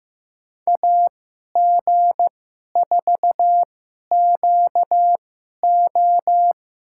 Morse code, heard as a keyed tone, about 15 wpm, spaced as for 11 wpm, 700 Hz. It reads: AG4QO
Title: USA Call Signs - 5 Characters - 2x Character Spacing - 15wpm